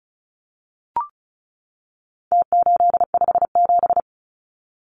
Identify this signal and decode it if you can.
Morse code, heard as a keyed tone, about 34 wpm, 700 Hz. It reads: T857
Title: Numbers - Contest Style - 4-Digit 35wpm